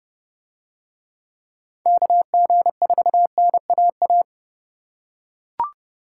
Morse code, heard as a keyed tone, about 28 wpm, 700 Hz. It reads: KG4NAA